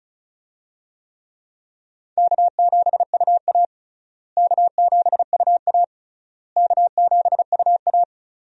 Morse code, heard as a keyed tone, about 36 wpm, 700 Hz. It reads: K7UA K7UA K7UA
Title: USA Call Signs - 4 Characters - Repeated 3x - 35wpm